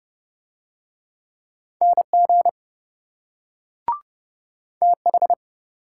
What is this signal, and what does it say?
Morse code, heard as a keyed tone, about 27 wpm, 700 Hz. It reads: NG TH